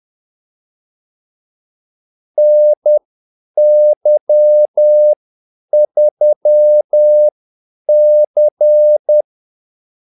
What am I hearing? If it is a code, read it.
Morse code, heard as a keyed tone, about 10 wpm, spaced as for 8 wpm, 600 Hz. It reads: NY3C
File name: Callsigns-10-8wpm